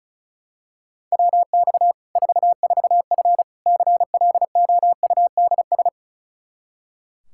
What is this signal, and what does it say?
Morse code, heard as a keyed tone, about 36 wpm, 700 Hz. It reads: WX 44F CLOUDS
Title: K1USN SST QSO Practice 35wpm